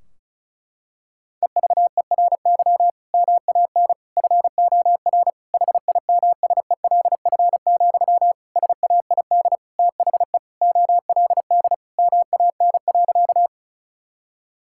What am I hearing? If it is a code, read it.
Morse code, heard as a keyed tone, about 36 wpm, 700 Hz. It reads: EVERY MAN FOR HIMSELF, SAID THE OLD MAN.
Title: Sentences from Top 1600 Words 35wpm